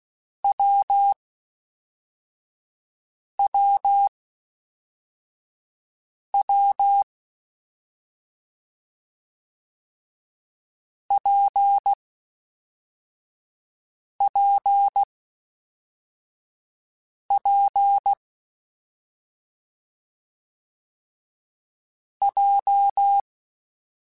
Morse code, heard as a keyed tone, about 16 wpm, 800 Hz. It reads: W W W P P P J
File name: Difficult Combos / Difficult Combos-mp3s / WPJ WPJ1 @5